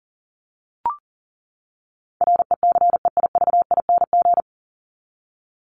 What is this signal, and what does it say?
Morse code, heard as a keyed tone, about 38 wpm, 700 Hz. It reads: RECEIVING
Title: ING Suffix - Top 100 Words with Base Words 40wpm